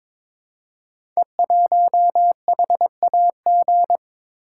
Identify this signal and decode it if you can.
Morse code, heard as a keyed tone, about 22 wpm, 700 Hz. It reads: E1HAG
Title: Code Groups - 5 Letter:Numbers - 22wpm